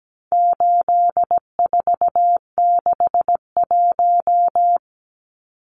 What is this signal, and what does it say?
Morse code, heard as a keyed tone, about 17 wpm, 700 Hz. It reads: 8461
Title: Numbers - 4-Digit 17wpm